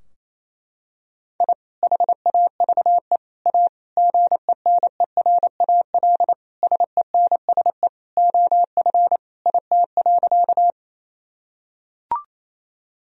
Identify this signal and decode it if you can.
Morse code, heard as a keyed tone, about 28 wpm, 700 Hz. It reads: I HAVE A GENERAL SENSE OF IT.
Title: Sentences from Top 1000 Words - Review Part 4 - 28wpm